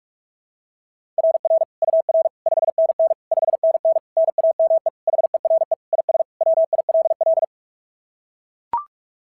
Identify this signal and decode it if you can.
Morse code, heard as a keyed tone, about 44 wpm, 690 Hz. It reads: RR UR 5NN 5NN NAME HERE IS WILL
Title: SOTA QSO Practice 45wpm